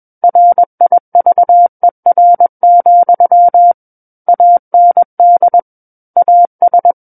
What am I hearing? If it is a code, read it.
Morse code, heard as a keyed tone, about 21 wpm, 700 Hz. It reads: RIVER, AND AS
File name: Book One / 12-Chapter 12 - 21 WPM